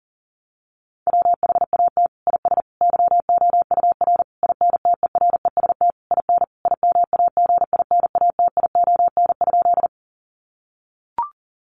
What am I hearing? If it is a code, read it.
Morse code, heard as a keyed tone, about 40 wpm, 700 Hz. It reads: WHAT IS YOUR INTEREST IN IMAGINATION?
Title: Sentences from Top 1300 Words 40wpm